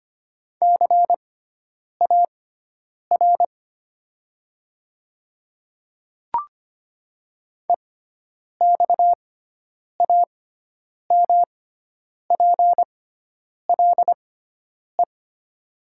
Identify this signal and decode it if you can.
Morse code, heard as a keyed tone, about 25 wpm, spaced as for 9 wpm, 700 Hz. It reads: CAR EXAMPLE
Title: 6x Character Spacing - Top 300 Words - 25wpm